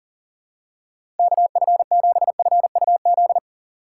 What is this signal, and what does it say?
Morse code, heard as a keyed tone, about 37 wpm, 700 Hz. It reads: KF7FUZ
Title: USA Call Signs - 6 Characters - 6x Character Spacing - 40wpm